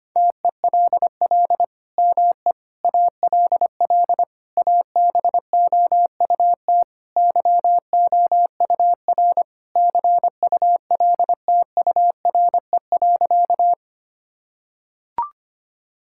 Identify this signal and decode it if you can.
Morse code, heard as a keyed tone, about 25 wpm, 700 Hz. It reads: TELL ME ALL ABOUT YOUR CULTURE.
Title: Sentences from Top 1000 Words - Review Part 6 - 25wpm